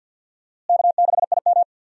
Morse code, heard as a keyed tone, about 51 wpm, 700 Hz. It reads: K6IM
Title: USA Call Signs - 4 Characters - Repeated 3x - 50wpm